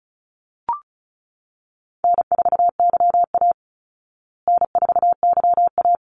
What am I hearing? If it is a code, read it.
Morse code, heard as a keyed tone, about 34 wpm, 700 Hz. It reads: N4YA N4YA
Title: Easy Call Signs Repeated 2x - 35wpm